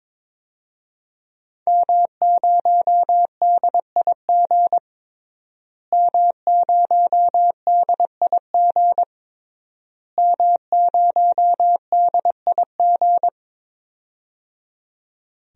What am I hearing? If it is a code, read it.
Morse code, heard as a keyed tone, about 22 wpm, 700 Hz. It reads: M0DIG M0DIG M0DIG
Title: DXCC-100 Call Signs - 5 Characters - Repeated 3x - 22wpm